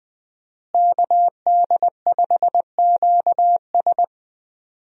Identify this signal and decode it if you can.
Morse code, heard as a keyed tone, about 20 wpm, 700 Hz. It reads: KD5QS
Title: USA Call Signs - 5 Characters - Repeated 2x - 20wpm